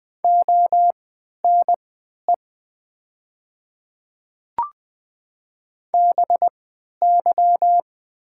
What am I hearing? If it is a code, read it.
Morse code, heard as a keyed tone, about 20 wpm, spaced as for 11 wpm, 700 Hz. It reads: ONE BY
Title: 3x Character Spacing - Top 100 Words - 20wpm